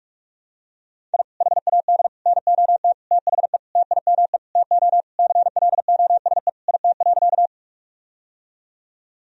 Morse code, heard as a keyed tone, about 46 wpm, 700 Hz. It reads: I HAD NOT THE TIME TO CLOSE IT.
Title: Sentences from Top 1000 Words - Review Part 1 - 45wpm